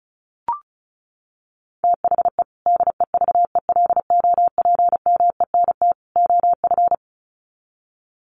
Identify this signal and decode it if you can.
Morse code, heard as a keyed tone, about 34 wpm, 700 Hz. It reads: THE DEVELOPMENT OF
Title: Sets of 3 Words 35wpm